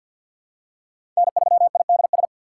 Morse code, heard as a keyed tone, about 47 wpm, 700 Hz. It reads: N3IDS
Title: USA Call Signs - 5 Characters - 4x Character Spacing - 50wpm